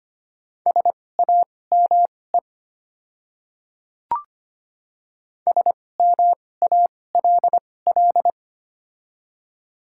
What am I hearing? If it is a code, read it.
Morse code, heard as a keyed tone, about 25 wpm, spaced as for 18 wpm, 700 Hz. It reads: SAME SMALL
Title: 2x Character Spacing - Top 200 Words - 25wpm